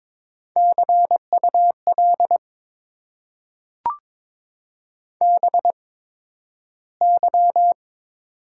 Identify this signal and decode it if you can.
Morse code, heard as a keyed tone, about 22 wpm, spaced as for 12 wpm, 700 Hz. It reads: CUL BY
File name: Top 90 QSO Elements - 8x Character Spacing - 22wpm